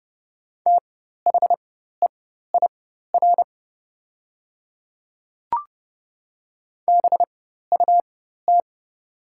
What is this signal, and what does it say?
Morse code, heard as a keyed tone, about 30 wpm, spaced as for 14 wpm, 700 Hz. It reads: THEIR BUT
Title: 4x Character Spacing - Top 100 Words - 30wpm